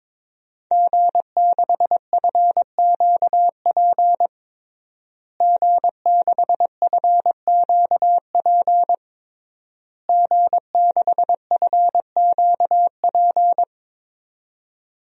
Morse code, heard as a keyed tone, about 22 wpm, 700 Hz. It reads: G6FQP G6FQP G6FQP
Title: DXCC-100 Call Signs - 5 Characters - Repeated 3x - 22wpm